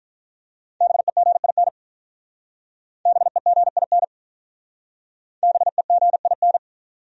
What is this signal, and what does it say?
Morse code, heard as a keyed tone, about 45 wpm, 700 Hz. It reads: BEGIN BEGIN BEGIN